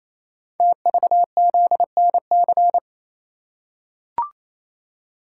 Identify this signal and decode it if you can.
Morse code, heard as a keyed tone, about 28 wpm, 700 Hz. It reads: TVZNC
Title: Code Groups - 5 Letter:Numbers - 28wpm